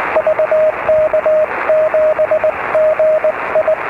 Morse code, heard as a keyed tone, about 20 wpm, 620 Hz, with radio noise. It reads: VK7GI